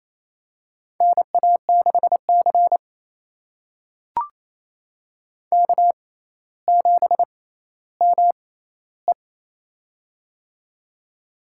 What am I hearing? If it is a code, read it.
Morse code, heard as a keyed tone, about 28 wpm, spaced as for 14 wpm, 700 Hz. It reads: NA6C K7ME
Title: USA Call Signs - 4 Characters - 6x Character Spacing - 28wpm